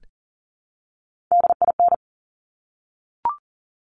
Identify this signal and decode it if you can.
Morse code, heard as a keyed tone, about 42 wpm, 690 Hz. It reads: DIN
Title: Top Three Letter Combinations in Words 40wpm